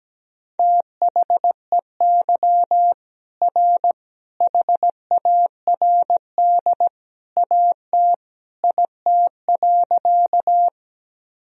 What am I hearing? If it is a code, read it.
Morse code, heard as a keyed tone, about 17 wpm, 700 Hz. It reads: THEY R HARD AT IT.
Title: Sentences from Top 300 Words and Abbreviations 17wpm